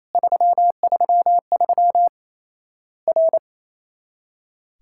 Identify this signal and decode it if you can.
Morse code, heard as a keyed tone, about 28 wpm, 700 Hz. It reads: 333 R
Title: Worked All Europe DX Contest - QTC Exchange - 28wpm